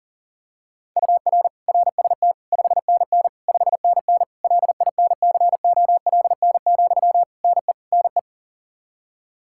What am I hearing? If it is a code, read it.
Morse code, heard as a keyed tone, about 40 wpm, 700 Hz. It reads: UR RST 5NN 5NN LINCOLN, NE NE
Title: Practice QSO - Exchange 1 - Beginning - 40wpm